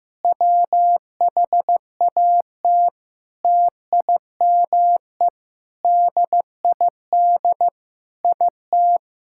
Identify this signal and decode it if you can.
Morse code, heard as a keyed tone, about 15 wpm, 700 Hz. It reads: WHAT TIME DID IT
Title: Sentences from Top 1000 Words - Review Part 6 - 15wpm